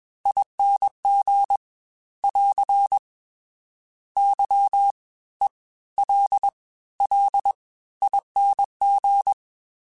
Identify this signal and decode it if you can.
Morse code, heard as a keyed tone, about 21 wpm, spaced as for 15 wpm, 800 Hz. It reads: ING <AR> YELLING